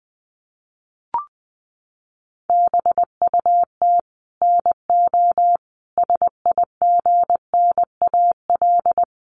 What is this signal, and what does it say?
Morse code, heard as a keyed tone, about 19 wpm, 700 Hz. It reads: BUT NO SIGNAL
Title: Sentences from Top 1400 Words 20wpm